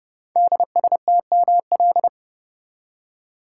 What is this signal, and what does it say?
Morse code, heard as a keyed tone, about 30 wpm, 700 Hz. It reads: DSTML
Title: Code Groups - 5 Letters - 30wpm